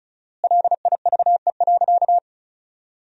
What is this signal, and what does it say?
Morse code, heard as a keyed tone, about 36 wpm, 700 Hz. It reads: LIVE.
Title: 4x Word-Spacing - Sentences from Top 300 Words - 35wpm